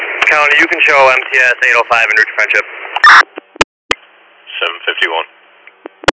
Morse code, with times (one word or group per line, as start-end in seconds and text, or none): none